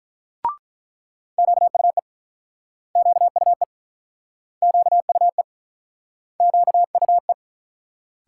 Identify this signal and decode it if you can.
Morse code, heard as a keyed tone, about 48 wpm, 700 Hz. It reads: QUE QUE QUE QUE